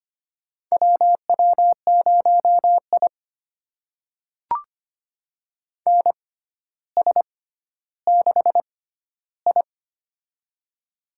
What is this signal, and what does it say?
Morse code, heard as a keyed tone, about 25 wpm, spaced as for 13 wpm, 700 Hz. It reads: WW0I NS6I
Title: USA Call Signs - 6x Character Spacing - 25wpm